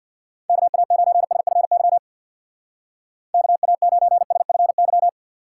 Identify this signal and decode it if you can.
Morse code, heard as a keyed tone, about 60 wpm, 700 Hz. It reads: KA9SFY KA9SFY
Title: Speed Racing - USA 2x3 Call Signs 50wpm